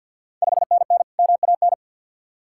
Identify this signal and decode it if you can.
Morse code, heard as a keyed tone, about 53 wpm, 700 Hz. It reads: 5NN MAN
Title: Ontario QSO Party Exchange 50wpm